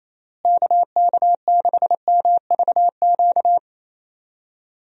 Morse code, heard as a keyed tone, about 27 wpm, 700 Hz. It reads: KK6MVQ
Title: USA Call Signs - 6 Characters - Repeated 3x - 28wpm